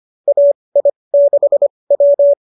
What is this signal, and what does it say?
Morse code, heard as a keyed tone, about 25 wpm, spaced as for 20 wpm, 590 Hz. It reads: AI6W